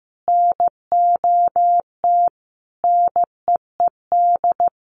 Morse code, heard as a keyed tone, about 15 wpm, 700 Hz. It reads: NOT NEED
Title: Sets of 4 Words 15wpm